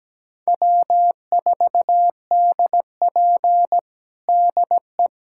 Morse code, heard as a keyed tone, about 17 wpm, 700 Hz. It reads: W4DP DE
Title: USA Callsign DE Callsign - 4 Characters - 17wpm